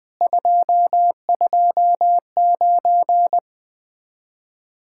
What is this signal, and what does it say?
Morse code, heard as a keyed tone, about 20 wpm, 700 Hz. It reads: 229
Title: Speed-Racing Common QSO Phrases Encore - 17wpm